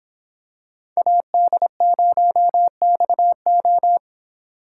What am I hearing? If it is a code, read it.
Morse code, heard as a keyed tone, about 26 wpm, 700 Hz. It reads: AD0XO